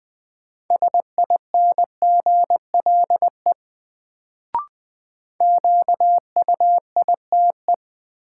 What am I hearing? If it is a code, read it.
Morse code, heard as a keyed tone, about 20 wpm, 700 Hz. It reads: SINGLE QUITE
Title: Sentences from Top 700 Words - New Word Drill - 20wpm